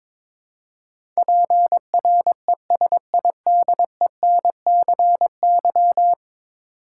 Morse code, heard as a keyed tone, about 22 wpm, 700 Hz. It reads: PRESIDENCY
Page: morse-code-ninja-practice-22wpm